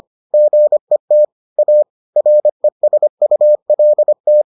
Morse code, heard as a keyed tone, about 25 wpm, 600 Hz. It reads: GET A RESULT